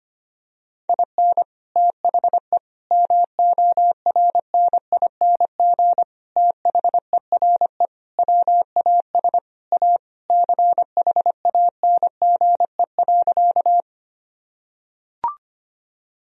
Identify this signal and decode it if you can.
Morse code, heard as a keyed tone, about 25 wpm, 700 Hz. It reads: IN THE MORNING THERE WAS A CHANGE.
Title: Sentences from Top 1000 Words - Review Part 1 - 25wpm